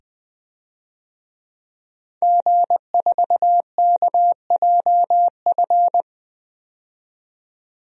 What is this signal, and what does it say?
Morse code, heard as a keyed tone, about 20 wpm, 700 Hz. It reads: G4KJF
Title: DXCC-100 Call Signs - 5 Characters - 20wpm